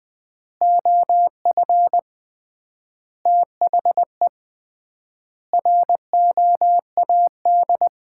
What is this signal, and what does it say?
Morse code, heard as a keyed tone, about 20 wpm, 700 Hz. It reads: OF THE ROAD
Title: 3x Word-Spacing - Sets of 5 Words - 20wpm